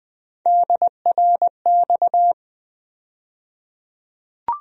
Morse code, heard as a keyed tone, about 20 wpm, 700 Hz. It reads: DRX
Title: USA Call Sign Postfixes - 3 Letters 20wpm